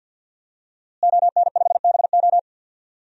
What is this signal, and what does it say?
Morse code, heard as a keyed tone, about 51 wpm, 700 Hz. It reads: ON5BO
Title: DXCC-100 Call Signs - 5 Characters - Repeated 2x - 50wpm